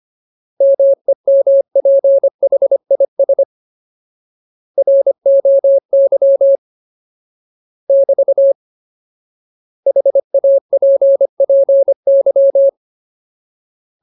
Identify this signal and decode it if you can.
Morse code, heard as a keyed tone, about 25 wpm, 550 Hz. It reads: MEMPHIS ROY <BT> HAPPY
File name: Ultimate Word List / 25WPM Ultimate